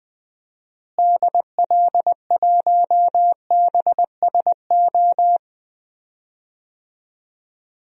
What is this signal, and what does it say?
Morse code, heard as a keyed tone, about 20 wpm, 700 Hz. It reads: DL1BSO
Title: DXCC-100 Call Signs - 6 Characters - Repeated 2x - 20wpm